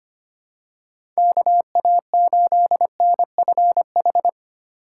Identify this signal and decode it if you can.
Morse code, heard as a keyed tone, about 25 wpm, 700 Hz. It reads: KA8NFH
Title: USA Call Signs - 4x Character Spacing - 25wpm